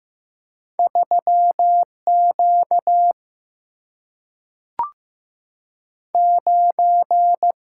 Morse code, heard as a keyed tone, about 15 wpm, 700 Hz. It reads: 3 Q 9
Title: Single-Letter Single-Number 15wpm